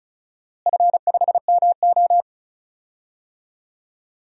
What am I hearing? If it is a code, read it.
Morse code, heard as a keyed tone, about 36 wpm, 700 Hz. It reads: F5MO
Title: DXCC-100 Call Signs - 4 Characters - Repeated 2x - 35wpm